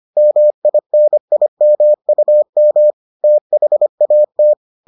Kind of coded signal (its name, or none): Morse code